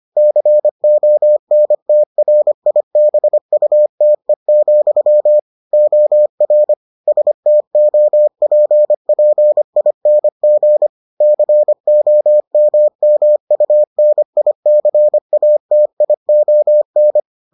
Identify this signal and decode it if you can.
Morse code, heard as a keyed tone, about 25 wpm, 600 Hz. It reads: CONTRIBUTE, OR STOPPING COMMUNICATION